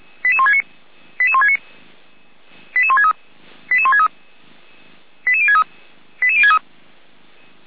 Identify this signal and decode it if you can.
Five-tone selective call (ZVEI), heard as quick runs of five tones, tones of about 70 ms. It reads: 88169 88169 88163 88163 88063 88063
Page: SDR/ZVEI